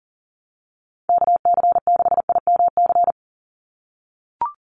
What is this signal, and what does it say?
Morse code, heard as a keyed tone, about 40 wpm, 700 Hz. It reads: KC6IMC
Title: USA Call Signs 40wpm